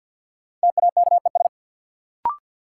Morse code, heard as a keyed tone, about 49 wpm, 700 Hz. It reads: TAKES
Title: Sets of 5 Words Encore - New Word Drill - 50wpm